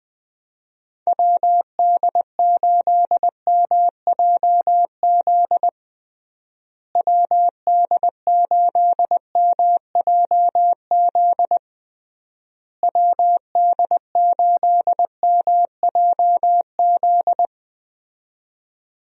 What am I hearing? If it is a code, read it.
Morse code, heard as a keyed tone, about 20 wpm, 700 Hz. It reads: WD8MJZ WD8MJZ WD8MJZ
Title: USA Call Signs - 6 Characters - Repeated 3x - 20wpm